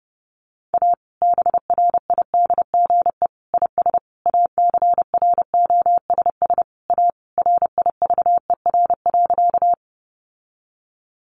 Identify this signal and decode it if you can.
Morse code, heard as a keyed tone, about 28 wpm, 700 Hz. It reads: A BRIDGE IS ACROSS A RIVER.
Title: Sentences from Top 1200 Words 30wpm